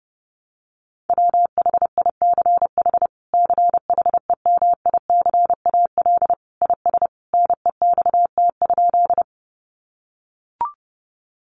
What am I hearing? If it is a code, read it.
Morse code, heard as a keyed tone, about 30 wpm, 700 Hz. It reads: WHICH CHEMICAL IS NEXT?
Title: Sentences from Top 1200 Words 30wpm